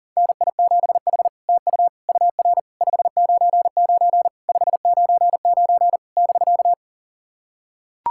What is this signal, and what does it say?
Morse code, heard as a keyed tone, about 40 wpm, 700 Hz. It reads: NI7H TU UR 599 599 <BK>
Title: SOTA QSO Practice 40wpm